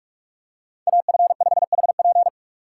Morse code, heard as a keyed tone, about 44 wpm, 700 Hz. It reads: AF5HP